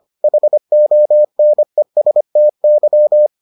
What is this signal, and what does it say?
Morse code, heard as a keyed tone, about 25 wpm, 600 Hz. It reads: HONESTY